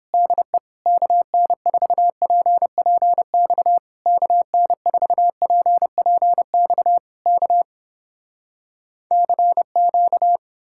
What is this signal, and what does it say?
Morse code, heard as a keyed tone, about 32 wpm, 700 Hz. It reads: DE KN4PPX KN4PPX K CQ